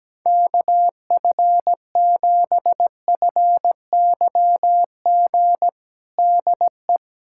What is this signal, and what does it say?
Morse code, heard as a keyed tone, about 17 wpm, 700 Hz. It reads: KF7FYG DE
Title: USA Callsign DE Callsign 17wpm